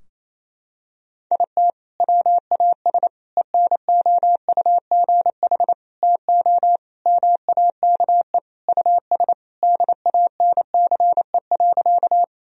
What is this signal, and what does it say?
Morse code, heard as a keyed tone, about 28 wpm, 700 Hz. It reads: IT WAS ENOUGH TO MAKE US DANCE.
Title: Sentences from Top 1000 Words - Review Part 4 - 28wpm